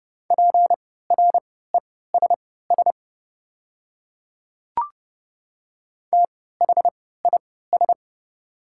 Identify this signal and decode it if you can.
Morse code, heard as a keyed tone, about 30 wpm, spaced as for 17 wpm, 690 Hz. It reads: PRESS THIS
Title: 3x Character Spacing - Top 300 Words - 30wpm